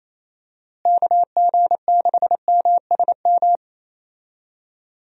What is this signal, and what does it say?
Morse code, heard as a keyed tone, about 28 wpm, 700 Hz. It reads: KG6MSM
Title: USA Call Signs - 6 Characters - Repeated 3x - 28wpm